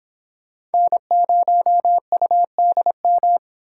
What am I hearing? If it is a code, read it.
Morse code, heard as a keyed tone, about 28 wpm, 700 Hz. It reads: N0UDM